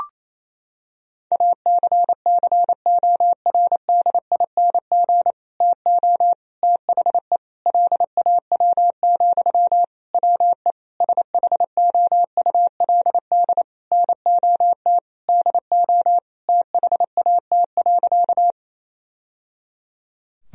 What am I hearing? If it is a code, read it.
Morse code, heard as a keyed tone, about 28 wpm, 700 Hz. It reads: ACCORDING TO THE LAW, WE SHOULD NOT DO THAT.
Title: Sentences from Top 1000 Words - Review Part 4 - 28wpm